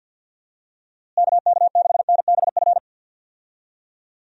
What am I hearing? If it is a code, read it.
Morse code, heard as a keyed tone, about 51 wpm, 700 Hz. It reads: KK6NBF
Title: USA Call Signs - 6 Characters - Repeated 2x - 50wpm